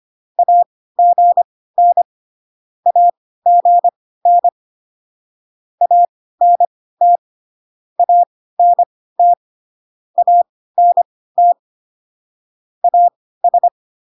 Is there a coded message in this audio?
Yes: Morse code from 0.4 to 13.7 s, heard as a keyed tone, about 25 wpm, spaced as for 16 wpm, 700 Hz. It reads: AGN AGN ANT ANT ANT AS